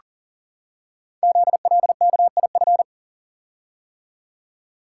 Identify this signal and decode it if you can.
Morse code, heard as a keyed tone, about 40 wpm, 700 Hz. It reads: ZLKIF
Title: Code Groups - 5 Letters - 40wpm